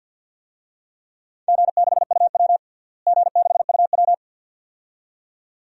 Morse code, heard as a keyed tone, about 47 wpm, 700 Hz. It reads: G6UW G6UW
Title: DXCC-100 Call Signs - 4 Characters - Repeated 2x - 50wpm